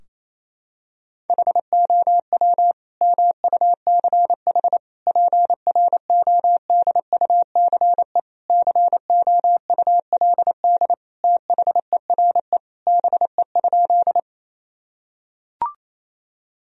Morse code, heard as a keyed tone, about 28 wpm, 700 Hz. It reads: HOW MUCH PRODUCE COULD THERE BE?
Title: Sentences from Top 1000 Words - Review Part 3 - 28wpm